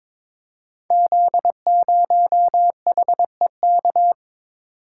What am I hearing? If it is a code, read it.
Morse code, heard as a keyed tone, about 22 wpm, 690 Hz. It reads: Z0HEK
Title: Code Groups - 5 Letter:Numbers - 22wpm